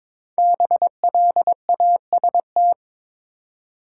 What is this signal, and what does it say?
Morse code, heard as a keyed tone, about 22 wpm, 700 Hz. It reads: BLAST
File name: 5 Letter Words - First 3 Letters Given - 22wpm